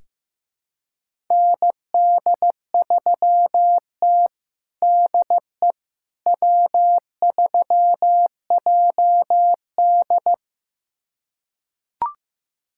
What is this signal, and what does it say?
Morse code, heard as a keyed tone, about 15 wpm, 700 Hz. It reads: ND3T DE W3JD
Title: USA Callsign DE Callsign - 4 Characters - 15wpm